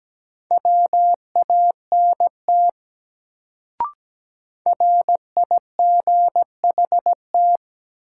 Morse code, heard as a keyed tone, about 17 wpm, 700 Hz. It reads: WANT RIGHT
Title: Sentences from Top 200 Words - New Word Drill - 17wpm